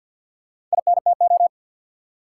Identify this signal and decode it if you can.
Morse code, heard as a keyed tone, about 53 wpm, 700 Hz. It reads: INTO
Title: Four Letter Words - 50wpm